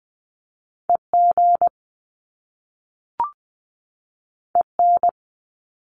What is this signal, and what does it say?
Morse code, heard as a keyed tone, about 20 wpm, 700 Hz. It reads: EG EN
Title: Top Two Letter Combinations in Words 20wpm